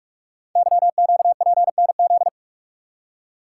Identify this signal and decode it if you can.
Morse code, heard as a keyed tone, about 44 wpm, 700 Hz. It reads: YQPNZ
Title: Code Groups - 5 Letters - 45wpm